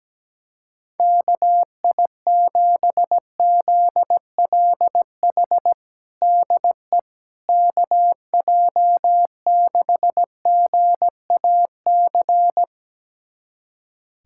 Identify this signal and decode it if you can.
Morse code, heard as a keyed tone, about 17 wpm, 700 Hz. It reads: KI7ZLH DE KJ6GAC
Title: USA Callsign DE Callsign - 6 Characters - 17wpm